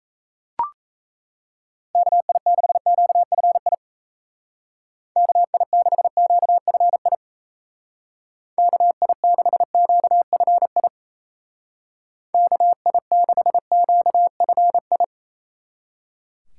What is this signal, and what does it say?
Morse code, heard as a keyed tone, about 37 wpm, 700 Hz. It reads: KI6QFI KI6QFI KI6QFI KI6QFI